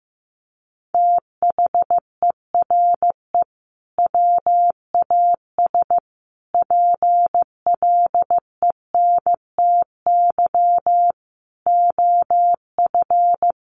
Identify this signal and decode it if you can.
Morse code, heard as a keyed tone, about 15 wpm, 700 Hz. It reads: THERE WAS PLENTY OF
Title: Sentences from Top 1300 Words 15wpm